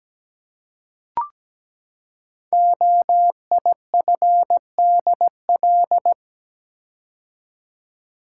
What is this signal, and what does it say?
Morse code, heard as a keyed tone, about 17 wpm, 700 Hz. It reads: OIFDL
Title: Code Groups - 5 Letters - 17wpm